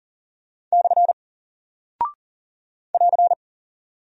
Morse code, heard as a keyed tone, about 40 wpm, 700 Hz. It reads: / <AR>
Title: Single Character (Letter:Number:Punctuation:Prosign) Alt 40wpm